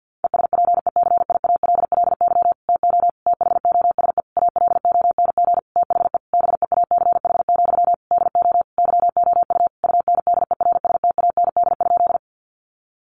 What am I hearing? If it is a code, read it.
Morse code, heard as a keyed tone, about 50 wpm, 700 Hz. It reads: ESPECIALLY TO THOSE ALONG THE BEACH, DO YOU UNDERSTAND?
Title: Sentences from Top 1500 Words 50wpm